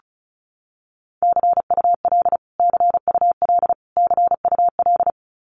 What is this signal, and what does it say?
Morse code, heard as a keyed tone, about 34 wpm, 700 Hz. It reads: CUL CUL CUL
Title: Abbreviations Repeated 3x - 35wpm